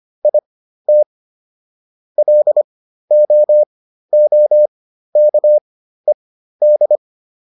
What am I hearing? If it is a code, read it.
Morse code, heard as a keyed tone, about 25 wpm, spaced as for 13 wpm, 600 Hz. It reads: IT LOOKED